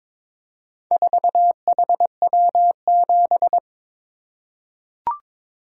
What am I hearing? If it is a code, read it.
Morse code, heard as a keyed tone, about 22 wpm, 700 Hz. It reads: 4HW7
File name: practice / 22wpm / Four Letter-Number Combinations - 22wpm